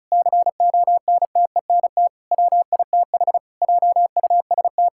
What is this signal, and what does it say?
Morse code, heard as a keyed tone, about 36 wpm, 690 Hz. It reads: CONTENT WITH JUST
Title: Sentences from Top 1000 Words - Review Part 6 - 35wpm